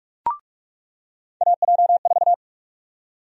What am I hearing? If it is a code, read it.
Morse code, heard as a keyed tone, about 42 wpm, 700 Hz. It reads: AJ4
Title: USA Call Sign Prefixes - 45wpm